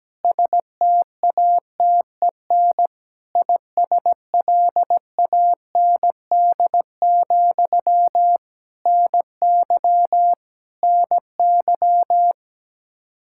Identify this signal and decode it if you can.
Morse code, heard as a keyed tone, about 17 wpm, 700 Hz. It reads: STATEN ISLAND, NY NY
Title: Practice QSO - Exchange 1 - Beginning - 17wpm